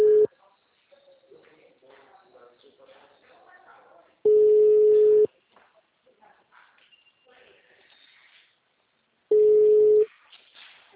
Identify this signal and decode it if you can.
Ringback tone (Europe, 425 Hz), with speech.